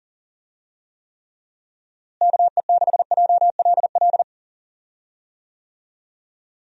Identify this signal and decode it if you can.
Morse code, heard as a keyed tone, about 40 wpm, 700 Hz. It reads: KE6JLL
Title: USA Call Signs - 6 Characters - Repeated 3x - 40wpm